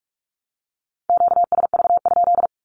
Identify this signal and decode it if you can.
Morse code, heard as a keyed tone, about 44 wpm, 700 Hz. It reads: QSV?
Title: Q-Codes 45wpm